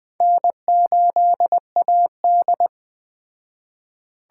Morse code, heard as a keyed tone, about 21 wpm, 700 Hz. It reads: N8AD